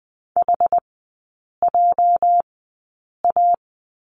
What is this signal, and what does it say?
Morse code, heard as a keyed tone, about 19 wpm, spaced as for 8 wpm, 700 Hz. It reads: HJA
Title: 205 Lesson 33e - Letter j - Three Characters 20wpm